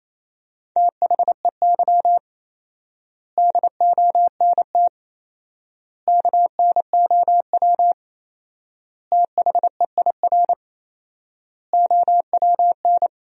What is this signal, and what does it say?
Morse code, heard as a keyed tone, about 28 wpm, 700 Hz. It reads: THEY DONT KNOW THEIR OWN
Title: 4x Word-Spacing - Sentences from Top 300 Words - 28wpm